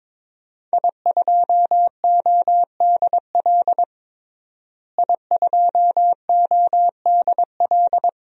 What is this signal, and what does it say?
Morse code, heard as a keyed tone, about 22 wpm, 700 Hz. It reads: I2ODL I2ODL
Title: DXCC-100 Call Signs - 5 Characters - Repeated 3x - 22wpm